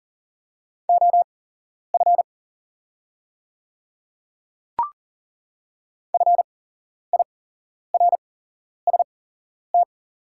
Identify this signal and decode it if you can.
Morse code, heard as a keyed tone, about 39 wpm, spaced as for 11 wpm, 700 Hz. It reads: OF FIRST
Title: 8x Character Spacing - Top 100 Words - 40wpm